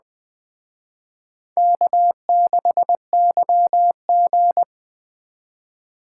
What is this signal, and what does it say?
Morse code, heard as a keyed tone, about 20 wpm, 700 Hz. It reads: K6YG
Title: Speed Racing - USA 1x2 Call Signs 17wpm